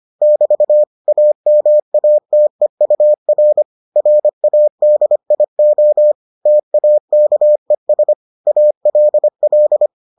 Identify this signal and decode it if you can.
Morse code, heard as a keyed tone, about 25 wpm, 600 Hz. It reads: <BT> AMATEUR RADIO TAKES ALL